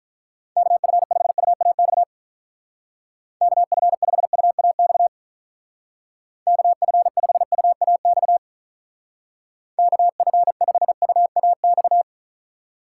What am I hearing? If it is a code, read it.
Morse code, heard as a keyed tone, about 43 wpm, 700 Hz. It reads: KF5UAX KF5UAX KF5UAX KF5UAX
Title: Speed Racing - USA 2x3 Call Signs 35wpm